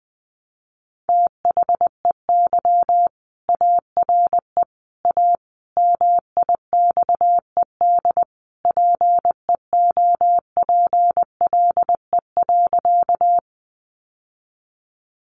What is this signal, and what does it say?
Morse code, heard as a keyed tone, about 20 wpm, 700 Hz. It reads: THEY ARE A MIXED PEOPLE.
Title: Sentences from Top 1500 Words 20wpm